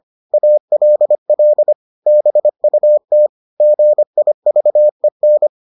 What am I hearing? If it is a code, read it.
Morse code, heard as a keyed tone, about 25 wpm, 600 Hz. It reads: ALL BUT GIVEN